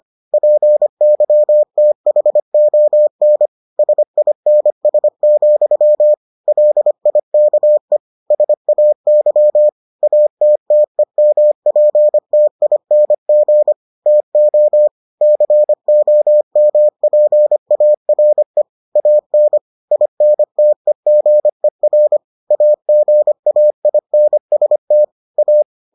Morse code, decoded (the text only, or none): PYTHON SINS, LIKE SAY ATTEMPTING TO COMPARE AN INTEGER AGAINST A